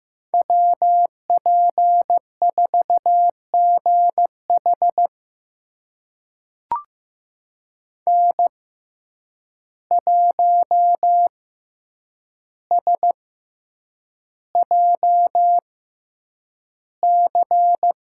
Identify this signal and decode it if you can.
Morse code, heard as a keyed tone, about 15 wpm, spaced as for 8 wpm, 700 Hz. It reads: WP4GH N1SJC
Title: USA Call Signs - 5 Characters - 6x Character Spacing - 15wpm